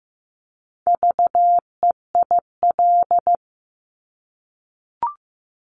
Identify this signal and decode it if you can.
Morse code, heard as a keyed tone, about 15 wpm, 700 Hz. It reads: VEIL
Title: Multi-letter Phonemes Repeated 3x 15wpm